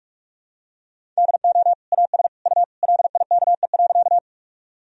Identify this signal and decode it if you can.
Morse code, heard as a keyed tone, about 41 wpm, 700 Hz. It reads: DO AS U LIKE.
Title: Sentences from Top 100 Words and Abbreviations 45wpm